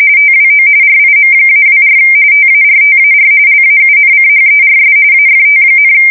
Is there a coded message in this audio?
Yes: RTTY from 0.2 to 5.9 s, about 45.45 baud. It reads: TWO RIM-24 TARTAR LAUNCHERS AFT,